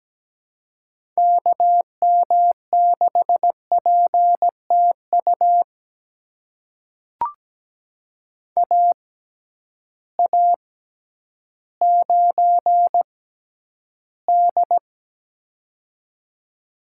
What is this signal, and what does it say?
Morse code, heard as a keyed tone, about 17 wpm, spaced as for 10 wpm, 700 Hz. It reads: KM6PTU AA9D
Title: USA Call Signs - 6x Character Spacing - 17wpm